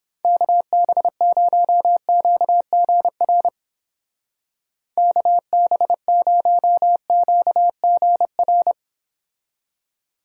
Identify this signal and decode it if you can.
Morse code, heard as a keyed tone, about 31 wpm, 700 Hz. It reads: KB0QGR KB0QGR